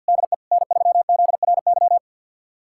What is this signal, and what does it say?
Morse code, heard as a keyed tone, about 51 wpm, 700 Hz. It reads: DE N3ZRY
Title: USA Callsign DE Callsign 50wpm